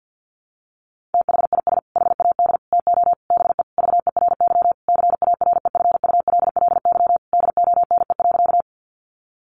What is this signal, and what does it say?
Morse code, heard as a keyed tone, about 50 wpm, 700 Hz. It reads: THIS HAD TO BE VERY CAREFULLY DONE.
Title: Sentences from Top 1300 Words 50wpm